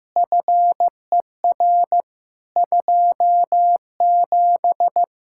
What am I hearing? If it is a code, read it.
Morse code, heard as a keyed tone, about 15 wpm, 700 Hz. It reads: FER 27
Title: K1USN SST QSO Practice 15wpm